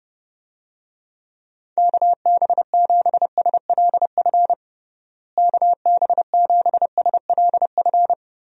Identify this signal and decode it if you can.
Morse code, heard as a keyed tone, about 32 wpm, 700 Hz. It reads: KB7SLF KB7SLF
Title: USA Call Signs - 6 Characters - Repeated 2x - 30wpm